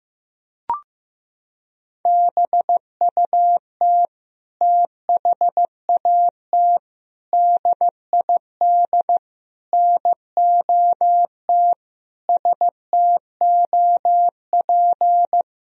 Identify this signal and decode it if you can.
Morse code, heard as a keyed tone, about 15 wpm, 700 Hz. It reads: BUT THAT DID NOT STOP
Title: Sentences from Top 1000 Words - Review Part 1 - 15wpm